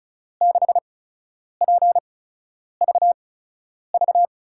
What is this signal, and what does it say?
Morse code, heard as a keyed tone, about 36 wpm, spaced as for 10 wpm, 700 Hz. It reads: 6PVV
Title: USA Call Signs - 6 Characters - 8x Character Spacing - 35wpm